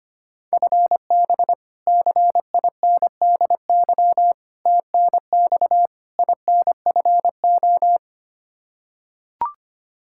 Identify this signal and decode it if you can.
Morse code, heard as a keyed tone, about 25 wpm, 700 Hz. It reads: FB CINDY TNX INFO
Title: Practice QSO - Exchange 3 - More Info - 25wpm